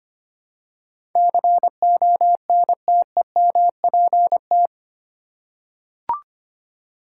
Morse code, heard as a keyed tone, about 25 wpm, 700 Hz. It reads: CONTEMPT
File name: Eight Letter Words - 25wpm